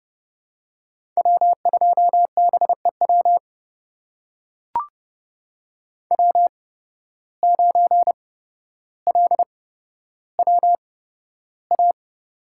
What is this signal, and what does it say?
Morse code, heard as a keyed tone, about 32 wpm, spaced as for 13 wpm, 700 Hz. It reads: W2BEW W9LWA
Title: USA Call Signs - 5 Characters - 8x Character Spacing - 30wpm